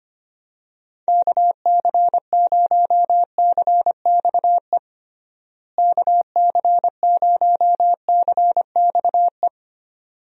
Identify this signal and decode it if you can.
Morse code, heard as a keyed tone, about 25 wpm, 700 Hz. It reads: KC0CXE KC0CXE
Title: USA Call Signs - 6 Characters - Repeated 2x - 25wpm